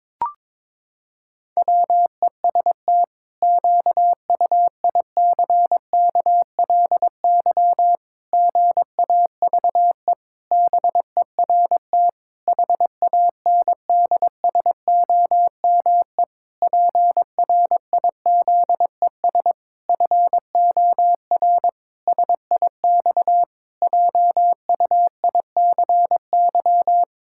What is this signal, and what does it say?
Morse code, heard as a keyed tone, about 22 wpm, 700 Hz. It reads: WEST QUICKLY GAVE BERT HANDSOME PRIZES FOR SIX JUICY
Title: Pangram Challenge v2 - 22wpm